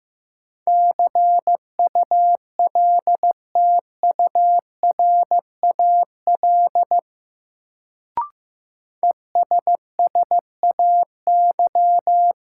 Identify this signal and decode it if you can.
Morse code, heard as a keyed tone, about 15 wpm, 700 Hz. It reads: CULTURAL ESSAY
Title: Sentences from Top 1500 Words - New Word Drill Alt - 15wpm